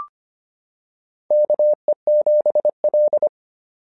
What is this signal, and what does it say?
Morse code, heard as a keyed tone, about 25 wpm, 620 Hz. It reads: KE7L
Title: SOTA QSO Practice 25wpm